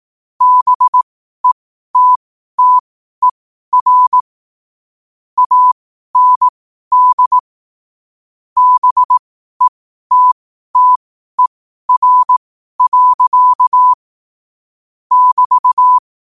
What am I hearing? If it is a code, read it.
Morse code, heard as a keyed tone, about 18 wpm, spaced as for 12 wpm, 1000 Hz. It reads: BETTER AND BETTER. <BT>
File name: qotdfiles / 12wpm